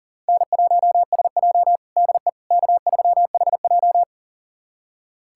Morse code, heard as a keyed tone, about 37 wpm, 700 Hz. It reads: N1SJ DE K3HJ
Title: USA Callsign DE Callsign - 4 Characters - 40wpm